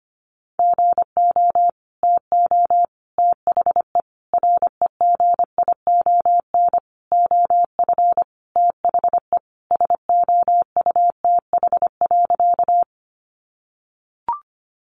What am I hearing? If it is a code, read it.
Morse code, heard as a keyed tone, about 25 wpm, 700 Hz. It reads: GO TO THE REGION OF THE SOUTH.
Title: Sentences from Top 1500 Words 25wpm